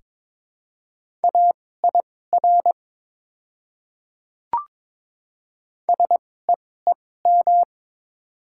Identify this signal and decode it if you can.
Morse code, heard as a keyed tone, about 22 wpm, spaced as for 16 wpm, 700 Hz. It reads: AIR SEEM
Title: 2x Character Spacing - Top 300 Words - 22wpm